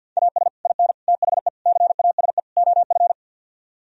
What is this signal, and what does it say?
Morse code, heard as a keyed tone, about 64 wpm, 700 Hz. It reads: AS IN THE CASE OF